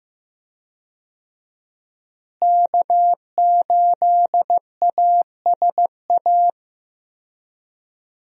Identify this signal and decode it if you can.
Morse code, heard as a keyed tone, about 15 wpm, 700 Hz. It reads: K8ASA
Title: USA Call Signs - 5 Characters - Repeated 3x - 15wpm